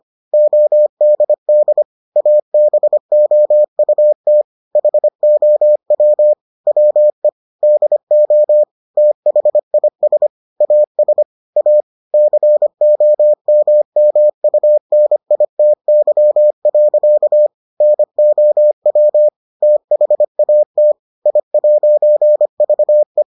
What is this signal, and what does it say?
Morse code, heard as a keyed tone, about 25 wpm, 600 Hz. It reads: ODD ABOUT HOW WE DO THIS AS A COMMUNITY. NOW THAT I'VE